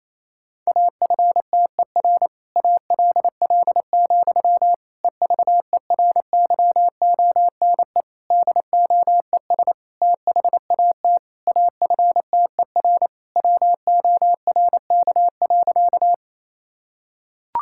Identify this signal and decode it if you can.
Morse code, heard as a keyed tone, about 27 wpm, 700 Hz. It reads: AFTER ALL, EVERYONE DOES THAT AFTER WORK.
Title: Sentences from Top 1000 Words - Review Part 5 - 28wpm